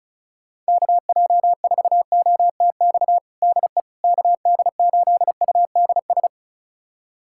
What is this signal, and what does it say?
Morse code, heard as a keyed tone, about 36 wpm, 700 Hz. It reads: KJ4OTX DE KD8UDS
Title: USA Callsign DE Callsign - 6 Characters - 35wpm